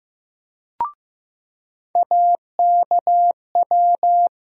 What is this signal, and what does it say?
Morse code, heard as a keyed tone, about 15 wpm, 700 Hz. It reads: AKW
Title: USA Call Sign Postfixes - 3 Letters 15wpm